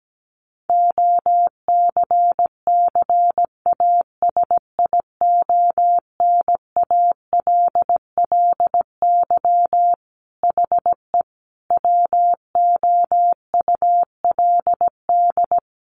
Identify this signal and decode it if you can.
Morse code, heard as a keyed tone, about 17 wpm, 700 Hz. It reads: OCCASIONALLY HE WOULD
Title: Sentences from Top 1300 Words 17wpm